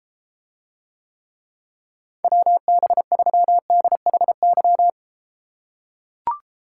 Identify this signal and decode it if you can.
Morse code, heard as a keyed tone, about 31 wpm, 700 Hz. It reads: WB3DHY